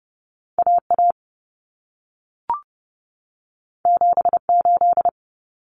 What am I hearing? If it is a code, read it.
Morse code, heard as a keyed tone, about 30 wpm, 700 Hz. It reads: AA 78
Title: Common Cut Numbers - 2-Digit 30wpm